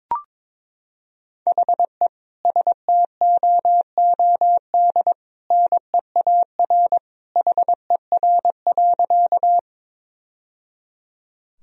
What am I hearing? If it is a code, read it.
Morse code, heard as a keyed tone, about 22 wpm, 700 Hz. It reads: HE STOOD NEAR HER.
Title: Sentences from Top 1000 Words - Review Part 2 - 22wpm